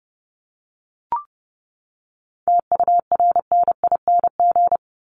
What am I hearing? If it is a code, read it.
Morse code, heard as a keyed tone, about 30 wpm, 700 Hz. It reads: TURNING
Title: ING Suffix - Top 100 Words with Base Words 30wpm